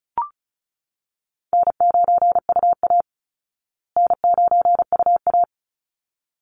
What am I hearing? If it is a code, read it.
Morse code, heard as a keyed tone, about 34 wpm, 700 Hz. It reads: N9UA N9UA
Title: Call Signs Repeated 3x - 35wpm